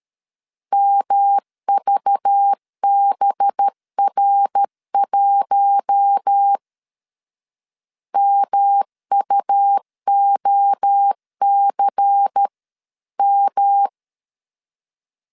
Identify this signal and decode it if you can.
Morse code, heard as a keyed tone, about 13 wpm, 800 Hz. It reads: MVBR1 MUOCM
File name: CW / DET / 1 6 1500C S B 12wpm 3431